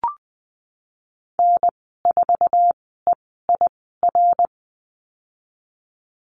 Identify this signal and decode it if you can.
Morse code, heard as a keyed tone, about 19 wpm, spaced as for 14 wpm, 700 Hz. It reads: N4EIR
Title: 245 Lesson 39g - Letter z - Call Signs 20wpm